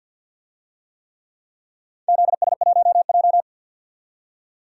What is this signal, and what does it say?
Morse code, heard as a keyed tone, about 51 wpm, 700 Hz. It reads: ZS1J
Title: DXCC-100 Call Signs - 4 Characters - 50wpm